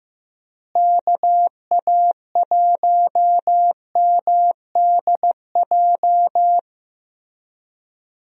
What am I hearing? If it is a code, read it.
Morse code, heard as a keyed tone, about 15 wpm, 700 Hz. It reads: KA1MDJ
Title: USA Call Signs - 6 Characters - Repeated 3x - 15wpm